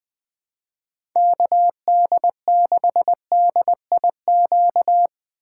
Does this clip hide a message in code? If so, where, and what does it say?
Yes: Morse code from 1.2 to 5.1 s, heard as a keyed tone, about 21 wpm, 700 Hz. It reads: KD6DIQ